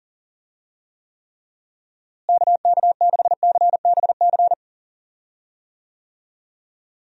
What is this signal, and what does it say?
Morse code, heard as a keyed tone, about 38 wpm, 700 Hz. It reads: KK6CBC
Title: USA Call Signs - 6 Characters - Repeated 3x - 40wpm